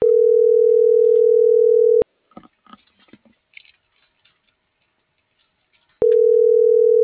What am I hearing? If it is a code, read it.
Ringback tone (North America), 440 + 480 Hz.